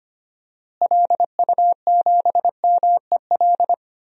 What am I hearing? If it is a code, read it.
Morse code, heard as a keyed tone, about 25 wpm, 700 Hz. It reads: LU7MEL
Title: DXCC-100 Call Signs - 6 Characters - Repeated 3x - 25wpm